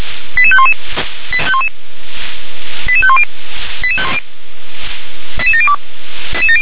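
Five-tone selective call (ZVEI), heard as quick runs of five tones, tones of about 70 ms. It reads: 88511 88510 88702
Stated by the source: ZVEI five-tone selective call received with an SDR.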